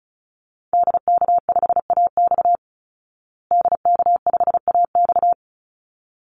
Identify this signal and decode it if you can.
Morse code, heard as a keyed tone, about 34 wpm, 700 Hz. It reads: DK5AX DK5AX
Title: Call Signs Repeated 3x - 35wpm